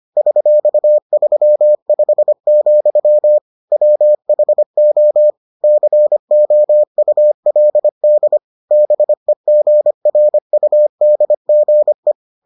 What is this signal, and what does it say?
Morse code, heard as a keyed tone, about 25 wpm, 600 Hz. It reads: $35, WHO COULD BEGRUDGE